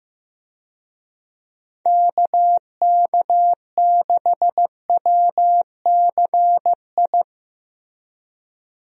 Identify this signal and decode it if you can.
Morse code, heard as a keyed tone, about 15 wpm, 700 Hz. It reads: KK6WCI
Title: USA Call Signs - 6 Characters - Repeated 2x - 15wpm